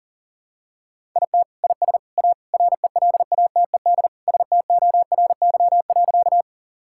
Morse code, heard as a keyed tone, about 37 wpm, 700 Hz. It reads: IT IS A RELATED STORY.
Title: Sentences from Top 1000 Words - Review Part 5 - 40wpm